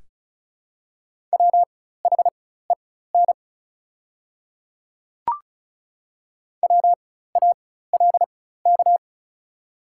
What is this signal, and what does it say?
Morse code, heard as a keyed tone, about 34 wpm, spaced as for 16 wpm, 690 Hz. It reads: WHEN WALK
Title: 4x Character Spacing - Top 300 Words - 35wpm